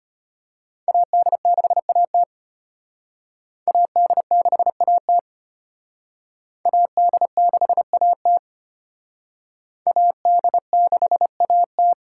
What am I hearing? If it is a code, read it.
Morse code, heard as a keyed tone, about 33 wpm, 700 Hz. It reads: AD6AT AD6AT AD6AT AD6AT